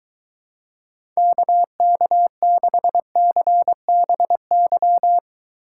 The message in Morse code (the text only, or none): KK6CBY